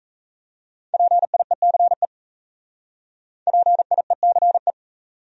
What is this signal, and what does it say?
Morse code, heard as a keyed tone, about 36 wpm, 700 Hz. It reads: PIECE PIECE